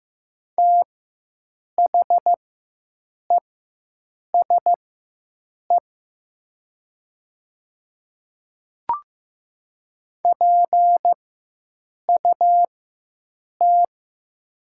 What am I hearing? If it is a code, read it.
Morse code, heard as a keyed tone, about 15 wpm, spaced as for 7 wpm, 700 Hz. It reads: THESE PUT
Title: 4x Character Spacing - Top 200 Words - 15wpm